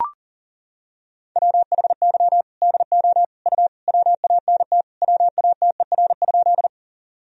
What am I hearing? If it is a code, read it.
Morse code, heard as a keyed tone, about 40 wpm, 700 Hz. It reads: WHY DO U WANT WATER?
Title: Sentences from Top 200 Words and Abbreviations 40wpm